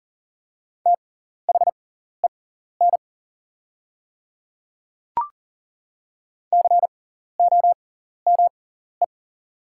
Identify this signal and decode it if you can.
Morse code, heard as a keyed tone, about 40 wpm, spaced as for 14 wpm, 700 Hz. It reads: THEN COME
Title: 6x Character Spacing - Top 100 Words - 40wpm